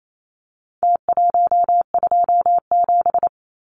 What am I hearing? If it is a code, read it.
Morse code, heard as a keyed tone, about 28 wpm, 700 Hz. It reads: T127
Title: Numbers - Contest Style - 4-Digit 28wpm